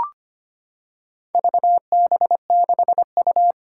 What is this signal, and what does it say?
Morse code, heard as a keyed tone, about 25 wpm, 700 Hz. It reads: VB6U
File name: practice / 25wpm / Four Letter-Number Combinations - 25wpm